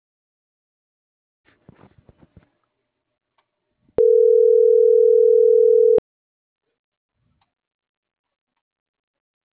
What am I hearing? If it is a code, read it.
Ringback tone (North America), 440 + 480 Hz.